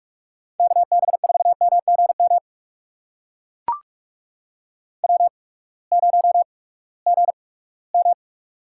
Morse code, heard as a keyed tone, about 44 wpm, spaced as for 22 wpm, 700 Hz. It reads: KB4MGM W0GM
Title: USA Call Signs - 8x Character Spacing - 45wpm